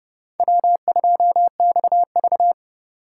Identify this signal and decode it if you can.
Morse code, heard as a keyed tone, about 30 wpm, 700 Hz. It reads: W2XV
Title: USA Call Signs - 4 Characters - 4x Character Spacing - 30wpm